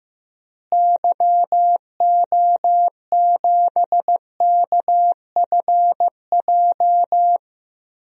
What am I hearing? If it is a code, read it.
Morse code, heard as a keyed tone, about 15 wpm, 700 Hz. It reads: YO7KFJ
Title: DXCC-100 Call Signs - 6 Characters - Repeated 2x - 15wpm